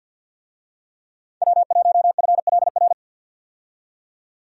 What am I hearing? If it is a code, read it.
Morse code, heard as a keyed tone, about 51 wpm, 700 Hz. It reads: W1FLR
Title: USA Call Signs - 5 Characters - 50wpm